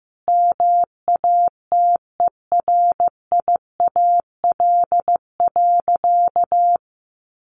Morse code, heard as a keyed tone, about 15 wpm, 700 Hz. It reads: MATERIAL.
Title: Sentences from Top 1100 Words 15wpm